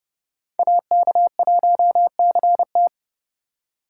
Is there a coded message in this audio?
Yes: Morse code from 0.6 to 2.9 s, heard as a keyed tone, about 29 wpm, 700 Hz. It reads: AK1CT